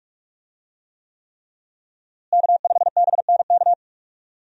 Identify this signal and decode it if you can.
Morse code, heard as a keyed tone, about 46 wpm, 700 Hz. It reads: K5BNK
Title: USA Call Signs - 5 Characters - Repeated 3x - 45wpm